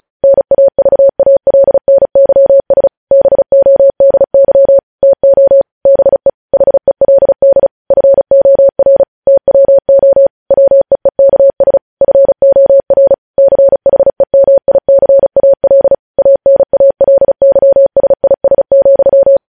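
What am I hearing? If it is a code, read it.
Morse code, heard as a keyed tone, about 36 wpm, 550 Hz. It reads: NAVALNYS BODY TO BE HELD FOR TWO WEEKS FOR CHEMICAL ANALYSIS,
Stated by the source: News headlines sent in Morse at 35 wpm.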